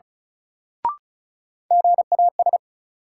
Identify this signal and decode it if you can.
Morse code, heard as a keyed tone, about 34 wpm, 700 Hz. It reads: GAS
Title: Sentences from Top 1100 Words - New Word Drill - 35wpm